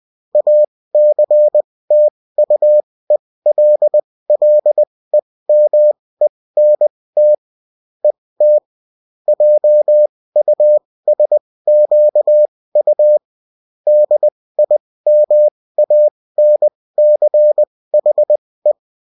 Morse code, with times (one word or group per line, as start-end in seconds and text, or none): 0.3-7.3 ACTUELLEMENT
8.0-8.6 ET
9.3-13.2 JUSQU
13.9-18.7 DIMANCHE